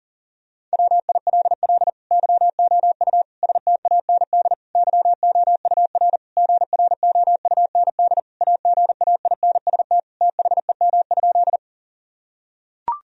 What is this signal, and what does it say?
Morse code, heard as a keyed tone, about 40 wpm, 700 Hz. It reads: WILL YOU STAND YOUR GROUND AGAINST THEM?
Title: Sentences from Top 1000 Words - Review Part 2 - 40wpm